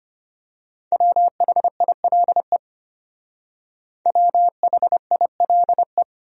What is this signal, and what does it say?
Morse code, heard as a keyed tone, about 29 wpm, 700 Hz. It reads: WHILE WHILE
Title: Speed-Racing Top 500 Words 25wpm